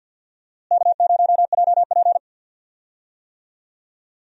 Morse code, heard as a keyed tone, about 53 wpm, 700 Hz. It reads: K0JP
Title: USA Call Signs - 4 Characters - Repeated 3x - 50wpm